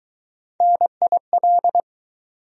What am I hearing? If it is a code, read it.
Morse code, heard as a keyed tone, about 20 wpm, 700 Hz. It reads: NIL